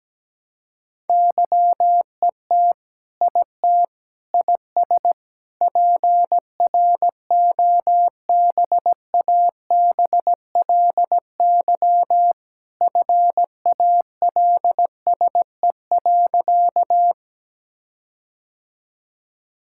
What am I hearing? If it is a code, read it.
Morse code, heard as a keyed tone, about 17 wpm, 700 Hz. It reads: YET IT IS PROBABLY FALSE.
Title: Sentences from Top 1600 Words 17wpm